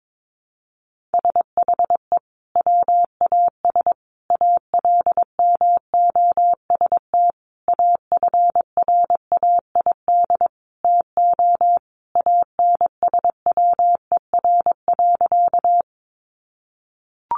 Morse code, heard as a keyed tone, about 22 wpm, 700 Hz. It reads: SHE WAS ALMOST AFRAID TO ANSWER.
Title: Sentences from Top 1500 Words 22wpm